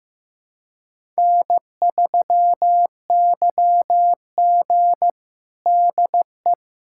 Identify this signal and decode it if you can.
Morse code, heard as a keyed tone, about 15 wpm, 700 Hz. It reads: N3YG DE
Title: USA Callsign DE Callsign - 4 Characters - 15wpm